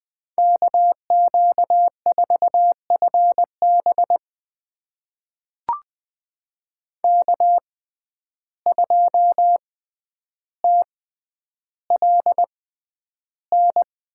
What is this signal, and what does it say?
Morse code, heard as a keyed tone, about 20 wpm, spaced as for 10 wpm, 700 Hz. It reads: KQ4FB K2TLN
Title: USA Call Signs - 5 Characters - 6x Character Spacing - 20wpm